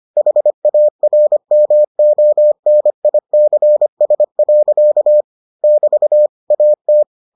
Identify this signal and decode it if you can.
Morse code, heard as a keyed tone, about 25 wpm, 600 Hz. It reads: HARMONICS. <BT> AT